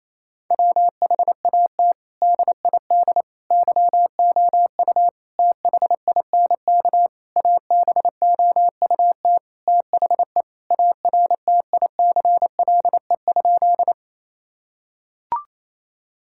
Morse code, heard as a keyed tone, about 28 wpm, 700 Hz. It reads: WHAT DID YOU THINK ABOUT THE ARTICLE?
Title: Sentences from Top 1000 Words - Review Part 5 - 28wpm